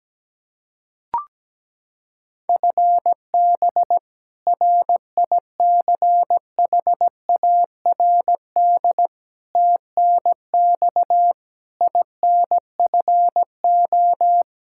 Morse code, heard as a keyed tone, about 17 wpm, 700 Hz. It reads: FB RICHARD TNX INFO
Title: Practice QSO - Exchange 3 - More Info - 17wpm